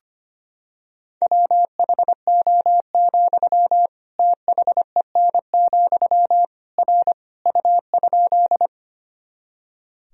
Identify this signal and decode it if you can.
Morse code, heard as a keyed tone, about 25 wpm, 700 Hz. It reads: WHO, THEN, R U?
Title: Sentences from Top 100 Words and Abbreviations 25wpm